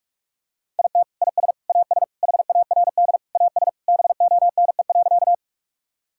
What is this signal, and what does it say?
Morse code, heard as a keyed tone, about 47 wpm, 700 Hz. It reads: IT IS AS HARD AS BONE.
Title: Sentences from Top 1600 Words 45wpm